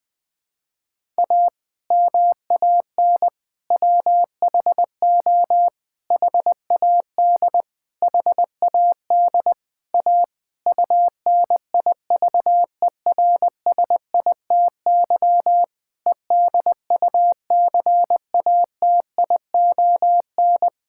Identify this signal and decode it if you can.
Morse code, heard as a keyed tone, about 20 wpm, 700 Hz. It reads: A MAN WHO HAD HAD A UNIVERSITY EDUCATION
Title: Sentences from Top 1000 Words - Review Part 6 - 20wpm